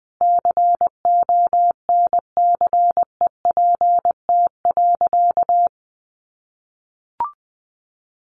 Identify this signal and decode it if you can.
Morse code, heard as a keyed tone, about 20 wpm, 700 Hz. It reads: CONCEPT.
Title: Sentences from Top 1100 Words 20wpm